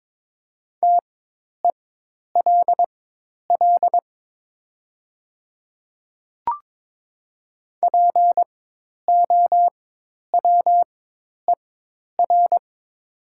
Morse code, heard as a keyed tone, about 22 wpm, spaced as for 10 wpm, 700 Hz. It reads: TELL POWER
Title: 4x Character Spacing - Top 500 Words - 22wpm